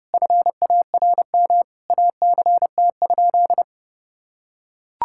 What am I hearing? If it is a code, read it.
Morse code, heard as a keyed tone, about 32 wpm, 700 Hz. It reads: FARM ACT?
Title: Sentences from Top 1000 Words - Review Part 5 - 30wpm